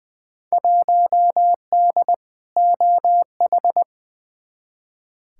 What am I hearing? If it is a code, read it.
Morse code, heard as a keyed tone, about 20 wpm, 700 Hz. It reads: 1D OH
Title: Field Day Exchanges - 20wpm